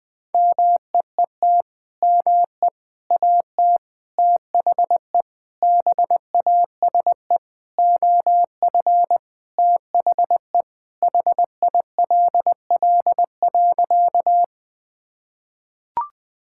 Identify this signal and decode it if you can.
Morse code, heard as a keyed tone, about 20 wpm, 700 Hz. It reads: MEET ME AT THE BASE OF THE HILL.
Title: Sentences with 5-Word Phrases - 20wpm